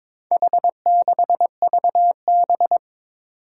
Morse code, heard as a keyed tone, about 23 wpm, 700 Hz. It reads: H6VB
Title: Four Challenging Letter-Number Combinations 22wpm